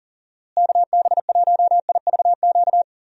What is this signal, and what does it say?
Morse code, heard as a keyed tone, about 40 wpm, 700 Hz. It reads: KB1IVQ
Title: USA Call Signs - 6 Characters - Repeated 3x - 40wpm